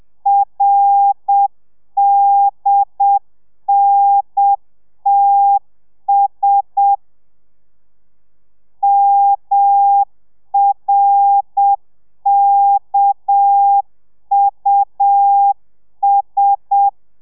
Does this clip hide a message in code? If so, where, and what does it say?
Yes: Morse code from 0.3 to 16.9 s, heard as a keyed tone, about 7 wpm, 800 Hz. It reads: RDNTS MRKUS